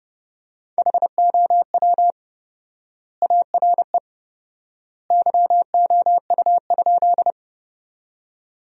Morse code, heard as a keyed tone, about 30 wpm, 700 Hz. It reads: HOW ARE YOU?
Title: 4x Word-Spacing - Sentences from Top 100 Words - 30wpm